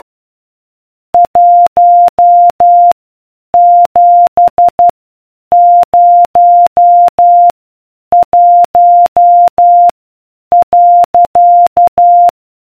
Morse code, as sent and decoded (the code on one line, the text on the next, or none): .---- --... ----- .---- .-.-.-
1701.